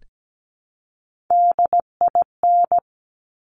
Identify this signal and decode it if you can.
Morse code, heard as a keyed tone, about 17 wpm, 700 Hz. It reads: DIN
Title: Top Three Letter Combinations in Words 17wpm